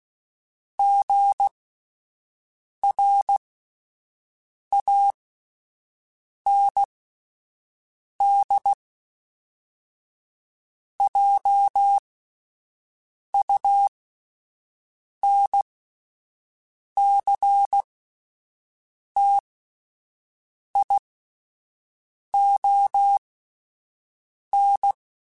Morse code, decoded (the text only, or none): GRANDJUNCTION